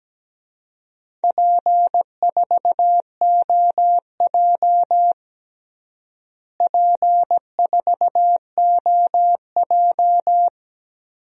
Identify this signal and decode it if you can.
Morse code, heard as a keyed tone, about 17 wpm, 700 Hz. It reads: P4OJ P4OJ
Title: DXCC-100 Call Signs - 4 Characters - Repeated 3x - 17wpm